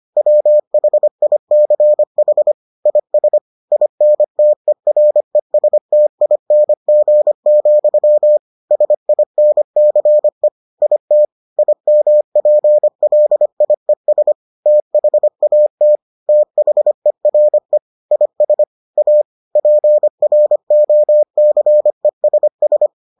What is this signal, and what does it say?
Morse code, heard as a keyed tone, about 25 wpm, 600 Hz. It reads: WHICH IS INTERESTING, SINCE IT IMPLIES THAT THERE IS A PROCESS